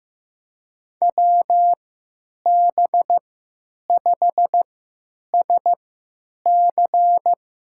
Morse code, heard as a keyed tone, about 15 wpm, spaced as for 9 wpm, 700 Hz. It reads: WB5SC
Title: USA Call Signs - 3x Character Spacing - 15wpm